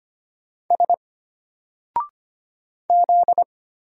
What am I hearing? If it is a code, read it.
Morse code, heard as a keyed tone, about 25 wpm, 700 Hz. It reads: S Z
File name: Single-Letter-Number-25wpm